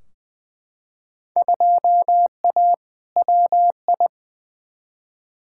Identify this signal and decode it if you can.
Morse code, heard as a keyed tone, about 20 wpm, 700 Hz. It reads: 2A WI
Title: Field Day Exchanges - 20wpm